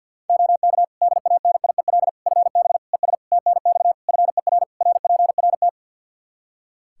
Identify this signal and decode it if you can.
Morse code, heard as a keyed tone, about 52 wpm, 700 Hz. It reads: OK DANIEL FB ES TNX FER RPRT
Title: Practice QSO - Exchange 2 - Info - 50wpm